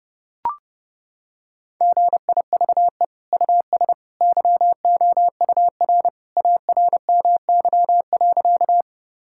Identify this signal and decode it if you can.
Morse code, heard as a keyed tone, about 28 wpm, 700 Hz. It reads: GIVE US YOUR ARMY.
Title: Sentences from Top 1000 Words - Review Part 5 - 30wpm